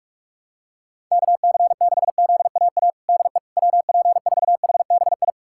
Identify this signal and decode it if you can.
Morse code, heard as a keyed tone, about 45 wpm, 690 Hz. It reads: KC6ZAA DE WP4HBI
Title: USA Callsign DE Callsign - 6 Characters - 45wpm